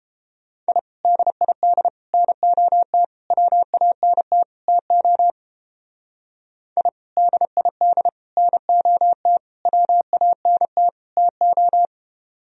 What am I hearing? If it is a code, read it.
Morse code, heard as a keyed tone, about 30 wpm, 700 Hz. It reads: I DID NOT WANT TO I DID NOT WANT TO